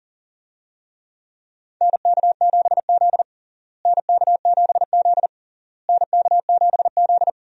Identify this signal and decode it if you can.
Morse code, heard as a keyed tone, about 40 wpm, 700 Hz. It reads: NK7Z NK7Z NK7Z
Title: USA Call Signs - 4 Characters - Repeated 3x - 40wpm